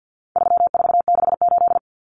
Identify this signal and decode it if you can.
Morse code, heard as a keyed tone, about 50 wpm, 700 Hz. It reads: 3468
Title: Numbers - 4-Digit 50wpm